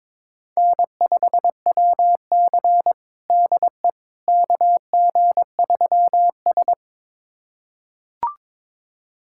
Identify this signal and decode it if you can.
Morse code, heard as a keyed tone, about 22 wpm, 700 Hz. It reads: N5WC DE KG3S
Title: USA Callsign DE Callsign - 4 Characters - 22wpm